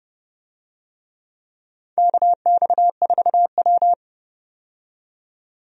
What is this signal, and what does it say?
Morse code, heard as a keyed tone, about 29 wpm, 700 Hz. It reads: KX4W
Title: USA Call Signs - 4 Characters - 30wpm